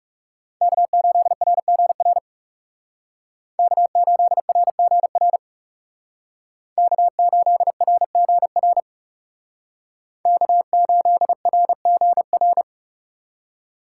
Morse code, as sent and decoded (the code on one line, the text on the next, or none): -.- ---.. .-. --. .-. / -.- ---.. .-. --. .-. / -.- ---.. .-. --. .-. / -.- ---.. .-. --. .-.
K8RGR K8RGR K8RGR K8RGR